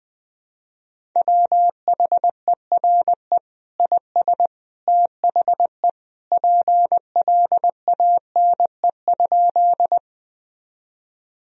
Morse code, heard as a keyed tone, about 20 wpm, 700 Hz. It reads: WHERE IS THE PLANE?
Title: Sentences from Top 1000 Words - Review Part 3 - 20wpm